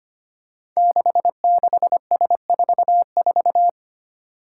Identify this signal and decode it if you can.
Morse code, heard as a keyed tone, about 25 wpm, 700 Hz. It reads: 66S44
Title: Five Challenging Letter-Number Combinations 25wpm